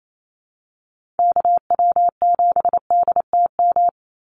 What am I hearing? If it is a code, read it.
Morse code, heard as a keyed tone, about 28 wpm, 700 Hz. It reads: KW7DTM
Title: USA Call Signs 28wpm